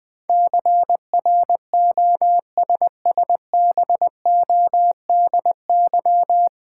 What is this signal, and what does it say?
Morse code, heard as a keyed tone, about 20 wpm, 700 Hz. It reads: CROSSBODY
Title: Difficult Words - Speed-Racing with Hints - 20wpm